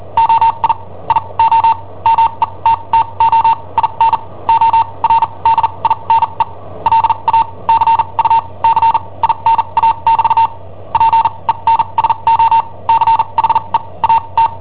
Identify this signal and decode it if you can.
Morse code, heard as a keyed tone, about 43 wpm, spaced as for 30 wpm, 910 Hz. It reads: OI IO METTO IN ORDINE LA CUCINA <BT> PENSO CHE A T
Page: Cassetta esercizi preparata da I7OHP per IW5ELP